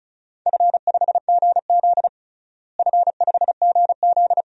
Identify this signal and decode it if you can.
Morse code, heard as a keyed tone, about 36 wpm, 700 Hz. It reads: F5GZ F5GZ
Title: DXCC-100 Call Signs - 4 Characters - Repeated 2x - 35wpm